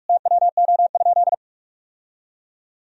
Morse code, heard as a keyed tone, about 46 wpm, 700 Hz. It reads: TWO?
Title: Sentences from Top 1000 Words - Review Part 2 - 45wpm